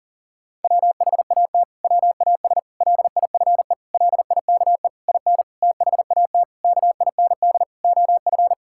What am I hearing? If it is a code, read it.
Morse code, heard as a keyed tone, about 40 wpm, 700 Hz. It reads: WHAT WAS LIFE LIKE IN THAT KIND OF
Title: Sentences from Top 1000 Words - Review Part 5 - 40wpm